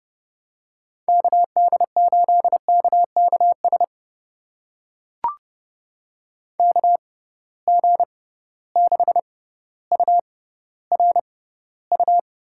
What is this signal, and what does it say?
Morse code, heard as a keyed tone, about 29 wpm, spaced as for 15 wpm, 700 Hz. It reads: KD8KKS KG6URU
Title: USA Call Signs - 6 Characters - 6x Character Spacing - 30wpm